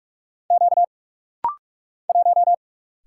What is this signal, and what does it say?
Morse code, heard as a keyed tone, about 44 wpm, 700 Hz. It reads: Q 1
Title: Single Letter-Number Alt - Rapid-Fire 45wpm